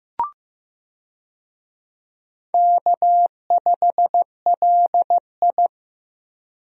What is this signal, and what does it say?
Morse code, heard as a keyed tone, about 15 wpm, 700 Hz. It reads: K5LI
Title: USA Call Signs - 4 Characters - Repeated 2x - 15wpm